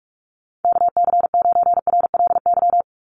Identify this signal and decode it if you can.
Morse code, heard as a keyed tone, about 44 wpm, 700 Hz. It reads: KC9RLY
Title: USA Call Signs 45wpm